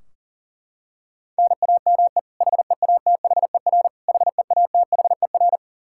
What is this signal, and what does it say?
Morse code, heard as a keyed tone, about 40 wpm, 700 Hz. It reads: NAME HEATHER HEATHER
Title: K1USN SST QSO Practice 40wpm